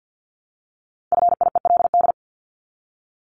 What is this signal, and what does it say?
Morse code, heard as a keyed tone, about 50 wpm, 700 Hz. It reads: FIELD
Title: Top 500 Words Repeated 50wpm